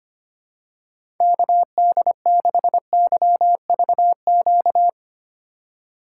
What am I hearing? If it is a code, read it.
Morse code, heard as a keyed tone, about 25 wpm, 700 Hz. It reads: KD6YVQ
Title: USA Call Signs - 6 Characters - 4x Character Spacing - 25wpm